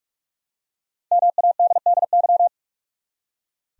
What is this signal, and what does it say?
Morse code, heard as a keyed tone, about 44 wpm, 700 Hz. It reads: MADDY
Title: USA Common Names - 45wpm